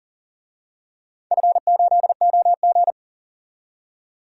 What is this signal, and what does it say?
Morse code, heard as a keyed tone, about 39 wpm, 700 Hz. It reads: F8OG
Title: DXCC-100 Call Signs - 4 Characters - 40wpm